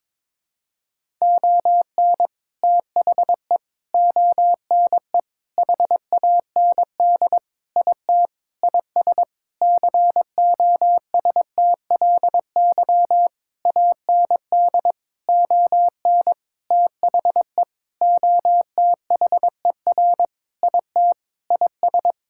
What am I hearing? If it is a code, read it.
Morse code, heard as a keyed tone, about 21 wpm, 700 Hz. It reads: ON THE ONE HAND IT IS COSTLY AND ON THE OTHER IT IS
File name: practice / 22wpm / Sentences with 8-Word Phrases - 22wpm